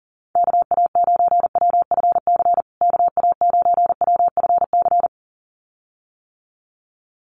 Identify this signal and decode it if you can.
Morse code, heard as a keyed tone, about 40 wpm, 700 Hz. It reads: KA9WFC KA9WFC
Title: USA Call Signs Repeated 2x 40wpm